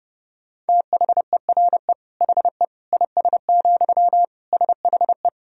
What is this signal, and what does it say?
Morse code, heard as a keyed tone, about 30 wpm, 700 Hz. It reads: THERE HE IS, SHE
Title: Sentences from Top 100 Words and Abbreviations 30wpm